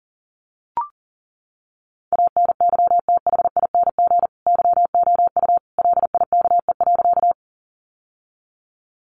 Morse code, heard as a keyed tone, about 37 wpm, 700 Hz. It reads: ANYTHING YOU LIKE.
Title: Sentences from Top 1000 Words 40wpm